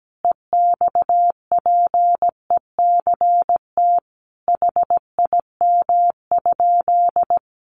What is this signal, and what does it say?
Morse code, heard as a keyed tone, about 17 wpm, 700 Hz. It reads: EXPECT HIM?
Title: Sentences from Top 1100 Words 17wpm